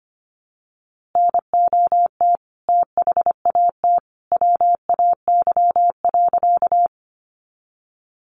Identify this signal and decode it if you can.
Morse code, heard as a keyed tone, about 25 wpm, 700 Hz. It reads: NOT THAT WAY.
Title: A Sound Of Thunder 25wpm